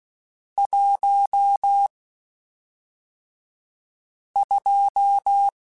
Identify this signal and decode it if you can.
Morse code, heard as a keyed tone, about 16 wpm, spaced as for 3 wpm, 800 Hz. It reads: 12
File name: Spaced Characters / Spaced Characters-mp3s / Weather5 22 sp.@5